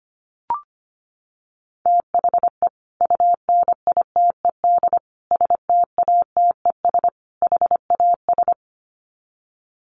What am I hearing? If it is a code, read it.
Morse code, heard as a keyed tone, about 25 wpm, 700 Hz. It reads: THE UNITED STATES HAS
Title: Sets of 4 Words 25wpm